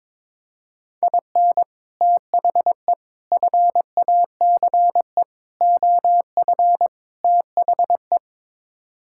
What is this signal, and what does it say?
Morse code, heard as a keyed tone, about 22 wpm, 700 Hz. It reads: IN THE FACE OF THE
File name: practice / 22wpm / Build A Phrase - 5-Word Phrases - 22wpm